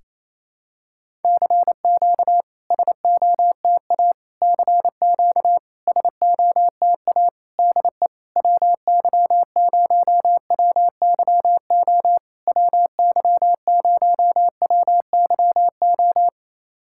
Morse code, heard as a keyed tone, about 28 wpm, 700 Hz. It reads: CQ SOTA CQ SOTA DE WY0WYO WY0WYO
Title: SOTA QSO Practice 28wpm